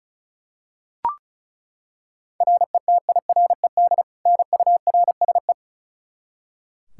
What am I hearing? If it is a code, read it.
Morse code, heard as a keyed tone, about 36 wpm, 700 Hz. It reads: RETIRED NURSE
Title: Practice QSO - Exchange 3 - More Info - 35wpm